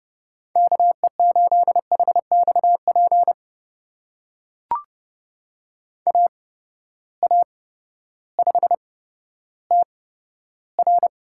Morse code, heard as a keyed tone, about 32 wpm, spaced as for 14 wpm, 700 Hz. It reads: KE8HXP AA5TR
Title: USA Call Signs - 8x Character Spacing - 30wpm